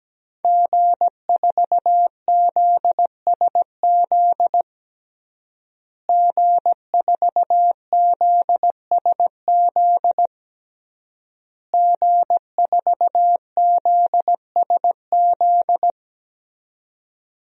Morse code, heard as a keyed tone, about 17 wpm, 700 Hz. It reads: G4ZSZ G4ZSZ G4ZSZ
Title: DXCC-100 Call Signs - 5 Characters - Repeated 3x - 17wpm